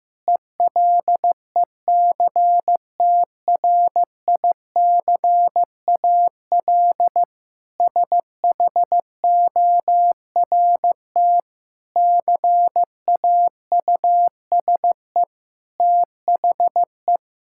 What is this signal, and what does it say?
Morse code, heard as a keyed tone, about 15 wpm, 700 Hz. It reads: ELECTRICAL SHORT CAUSE THE
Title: Sentences from Top 1000 Words - Review Part 6 - 15wpm